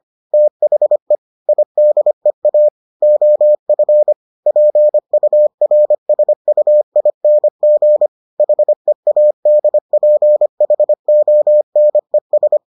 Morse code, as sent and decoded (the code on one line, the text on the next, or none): - .... / .. -.. .- / --- ..-. / .--. ..- .-. ... ..- .. -. --. / .... .- -.. .--. .... --- -. ...
THE IDEA OF PURSUING HEADPHONES